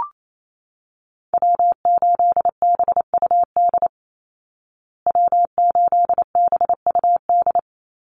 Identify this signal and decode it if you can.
Morse code, heard as a keyed tone, about 28 wpm, 700 Hz. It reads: W8BUD W8BUD
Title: Call Signs Repeated 2x - 28wpm